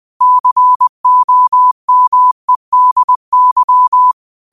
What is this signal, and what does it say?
Morse code, heard as a keyed tone, about 20 wpm, 1000 Hz. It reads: COMEDY